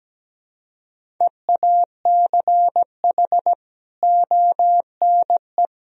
Morse code, heard as a keyed tone, about 17 wpm, 700 Hz. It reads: EACH ONE
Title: Sentences from Top 1000 Words - Review Part 1 - 17wpm